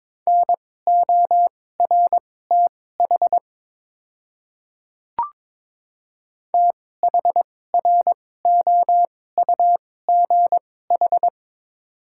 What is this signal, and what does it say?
Morse code, heard as a keyed tone, about 22 wpm, spaced as for 16 wpm, 700 Hz. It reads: NORTH THROUGH
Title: 2x Character Spacing - Top 500 Words - 22wpm